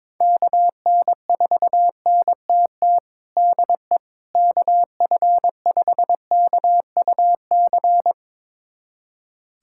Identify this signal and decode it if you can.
Morse code, heard as a keyed tone, about 21 wpm, 700 Hz. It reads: KN4NTT DE KF5KUC
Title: USA Callsign DE Callsign 22wpm